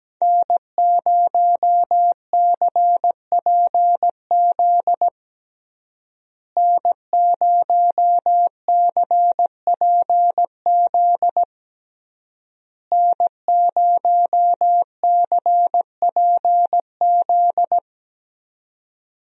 Morse code, heard as a keyed tone, about 17 wpm, 700 Hz. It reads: N0CPZ N0CPZ N0CPZ
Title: USA Call Signs - 5 Characters - Repeated 3x - 17wpm